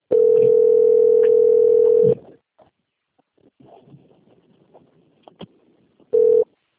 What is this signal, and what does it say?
Ringback tone (North America), 440 + 480 Hz.